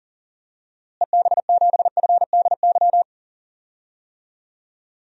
Morse code, heard as a keyed tone, about 40 wpm, 700 Hz. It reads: EB7FDY
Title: DXCC-100 Call Signs - 6 Characters - Repeated 3x - 40wpm